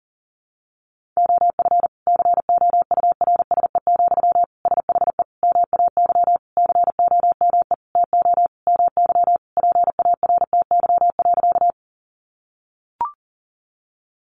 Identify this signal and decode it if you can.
Morse code, heard as a keyed tone, about 40 wpm, 700 Hz. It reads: OF COURSE, SHE MAY COME TO MY PARTY.
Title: Sentences from Top 1400 Words 40wpm